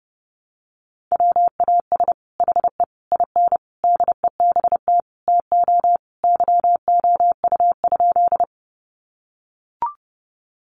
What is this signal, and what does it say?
Morse code, heard as a keyed tone, about 30 wpm, 700 Hz. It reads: WAS HE IN DEBT TO YOU?
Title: Sentences from Top 1300 Words 30wpm